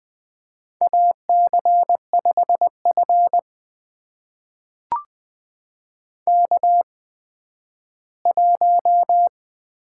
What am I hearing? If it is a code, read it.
Morse code, heard as a keyed tone, about 20 wpm, spaced as for 12 wpm, 700 Hz. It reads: AC5F K1
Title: USA Call Signs - 4 Characters - 8x Character Spacing - 20wpm